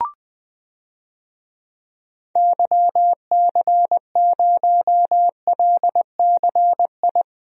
Morse code, heard as a keyed tone, about 20 wpm, 700 Hz. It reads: YC0LCI
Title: DXCC-100 Call Signs - 6 Characters - Repeated 3x - 20wpm